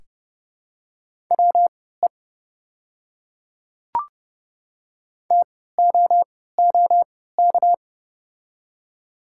Morse code, heard as a keyed tone, about 30 wpm, spaced as for 17 wpm, 700 Hz. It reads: WE TOOK
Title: 3x Character Spacing - Top 300 Words - 30wpm